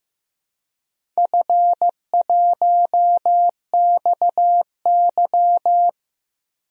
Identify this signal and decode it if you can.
Morse code, heard as a keyed tone, about 15 wpm, 700 Hz. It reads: F1XY
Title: DXCC-100 Call Signs - 4 Characters - 15wpm